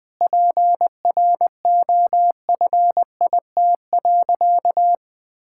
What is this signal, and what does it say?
Morse code, heard as a keyed tone, about 20 wpm, 700 Hz. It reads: PROFIT.
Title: Sentences from Top 1000 Words - Review Part 5 - 20wpm